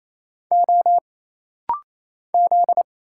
Morse code, heard as a keyed tone, about 27 wpm, 700 Hz. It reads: O Z
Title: Single Letters - Rapid-Fire 28wpm